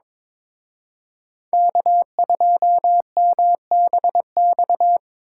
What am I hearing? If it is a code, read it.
Morse code, heard as a keyed tone, about 26 wpm, 700 Hz. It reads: K2MBX
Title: Speed Racing - USA 1x3 Call Signs 22wpm